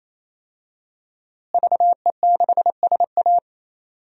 Morse code, heard as a keyed tone, about 27 wpm, 700 Hz. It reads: VE6SA
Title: DXCC-100 Call Signs - 5 Characters - Repeated 3x - 28wpm